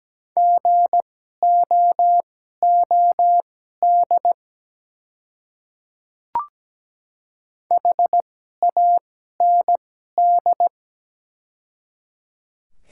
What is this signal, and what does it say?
Morse code, heard as a keyed tone, about 17 wpm, spaced as for 12 wpm, 700 Hz. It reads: GOOD HAND
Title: 2x Character Spacing - Top 200 Words - 17wpm